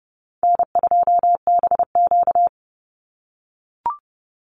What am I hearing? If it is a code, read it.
Morse code, heard as a keyed tone, about 30 wpm, 700 Hz. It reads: N2BQ
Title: USA Call Signs 30wpm